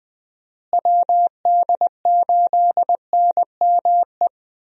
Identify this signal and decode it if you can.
Morse code, heard as a keyed tone, about 21 wpm, 700 Hz. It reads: WD8NME
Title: USA Call Signs - 6 Characters - Repeated 2x - 20wpm